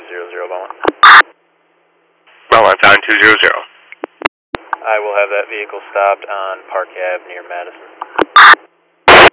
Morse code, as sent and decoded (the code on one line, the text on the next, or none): none